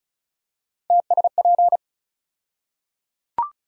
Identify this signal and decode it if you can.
Morse code, heard as a keyed tone, about 36 wpm, 700 Hz. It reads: TSP
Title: USA Call Sign Postfixes - 3 Letters 35wpm